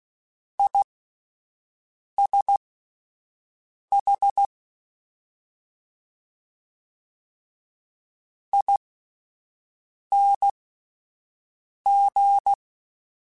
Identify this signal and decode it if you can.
Morse code, heard as a keyed tone, about 16 wpm, spaced as for 5 wpm, 800 Hz. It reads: ISH ING
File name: Misc / Misc-mp3s / Jumps @5